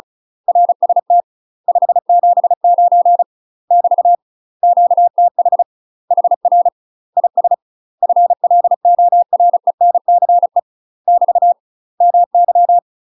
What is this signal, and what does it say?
Morse code, heard as a keyed tone, about 36 wpm, 700 Hz. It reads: RST 579 <BT> QTH HR IS FLORENCE <BT> MY
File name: UK MASTER QSOs TRAINING / 35WPM / UK MASTER QSOs 0027